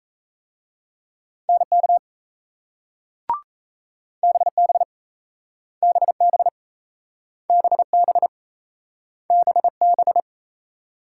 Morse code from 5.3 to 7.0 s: BB

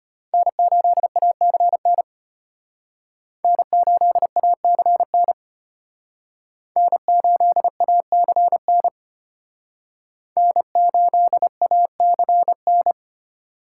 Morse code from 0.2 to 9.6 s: N8ACN N8ACN N8ACN

